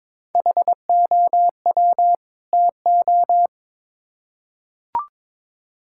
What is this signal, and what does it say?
Morse code, heard as a keyed tone, about 22 wpm, 700 Hz. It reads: HOW TO
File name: Build A Phrase - 2-Word Phrases - 22wpm